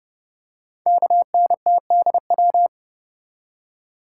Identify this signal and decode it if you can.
Morse code, heard as a keyed tone, about 30 wpm, 690 Hz. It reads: KNTDW
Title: Code Groups - 5 Letters - 30wpm